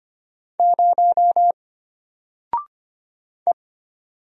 Morse code, heard as a keyed tone, about 25 wpm, 700 Hz. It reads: 0 E
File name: Single-Letter-Number-25wpm